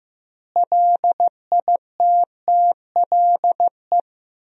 Morse code, heard as a keyed tone, about 15 wpm, 700 Hz. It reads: LITTLE
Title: 2x Word-Spacing - Sentences from Top 400 Words - 15wpm